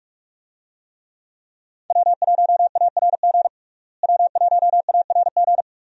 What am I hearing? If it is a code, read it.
Morse code, heard as a keyed tone, about 46 wpm, 700 Hz. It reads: W1ARG W1ARG
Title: USA Call Signs - 5 Characters - Repeated 2x - 45wpm